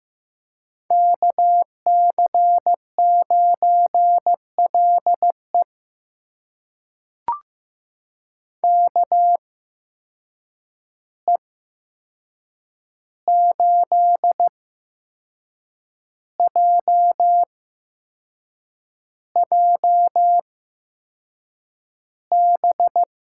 Morse code, heard as a keyed tone, about 15 wpm, spaced as for 6 wpm, 700 Hz. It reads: KC9LE KE8JJB
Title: USA Call Signs - 8x Character Spacing - 15wpm